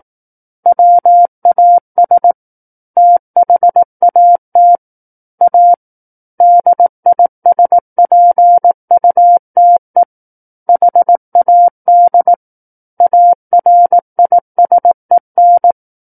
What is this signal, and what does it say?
Morse code, heard as a keyed tone, about 18 wpm, 700 Hz. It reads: WAS THAT A DISPUTE HAD ARISEN
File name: Book One / 09-Chapter 9 - 18 WPM